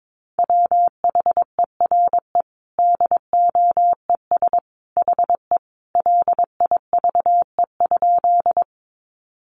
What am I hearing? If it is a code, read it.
Morse code, heard as a keyed tone, about 22 wpm, 700 Hz. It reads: WHERE DOES HE LIVE?
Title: Sentences from Top 200 Words 22wpm